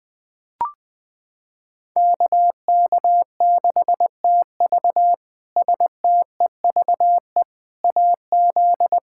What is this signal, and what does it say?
Morse code, heard as a keyed tone, about 21 wpm, 700 Hz. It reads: KK6TV STEVE AZ
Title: K1USN SST QSO Practice - 20wpm